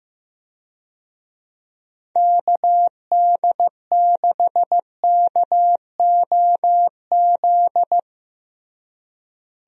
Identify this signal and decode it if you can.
Morse code, heard as a keyed tone, about 15 wpm, 700 Hz. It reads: KD6KOZ
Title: USA Call Signs - 6 Characters - Repeated 2x - 15wpm